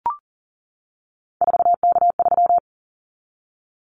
Morse code, heard as a keyed tone, about 38 wpm, 700 Hz. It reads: 4K3
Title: Callsign Prefixes for Top 100 DXCC - 40wpm